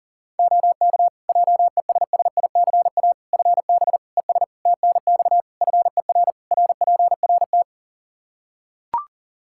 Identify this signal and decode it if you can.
Morse code, heard as a keyed tone, about 40 wpm, 700 Hz. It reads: OK JESSICA FB ES TNX FER RPRT
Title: Practice QSO - Exchange 2 - Info - 40wpm